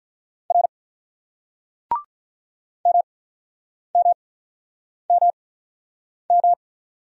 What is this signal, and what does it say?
Morse code, heard as a keyed tone, about 42 wpm, spaced as for 9 wpm, 700 Hz. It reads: R MMMM
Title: Speed-Racing Single Letters 35wpm